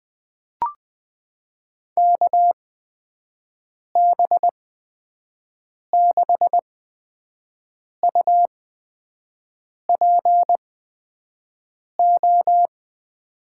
Morse code, heard as a keyed tone, about 20 wpm, spaced as for 5 wpm, 700 Hz. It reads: KB6UPO
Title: USA Call Signs - 6 Characters - 8x Character Spacing - 20wpm